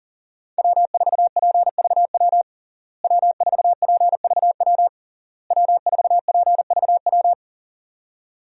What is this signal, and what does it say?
Morse code, heard as a keyed tone, about 40 wpm, 700 Hz. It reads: W4PVW W4PVW W4PVW
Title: USA Call Signs - 5 Characters - Repeated 3x - 40wpm